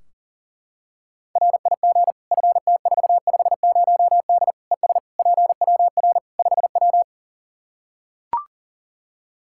Morse code, heard as a keyed tone, about 40 wpm, 700 Hz. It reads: RIG FT450D ES PWR 5W
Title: Practice QSO - Exchange 2 - Info - 40wpm